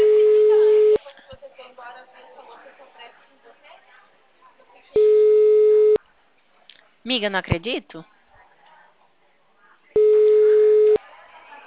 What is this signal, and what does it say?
Ringback tone (Europe, 425 Hz), with speech.